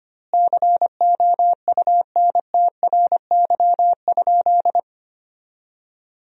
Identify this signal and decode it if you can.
Morse code, heard as a keyed tone, about 25 wpm, 700 Hz. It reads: COUNTRY?
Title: 2x Word-Spacing - Sentences from Top 300 Words - 25wpm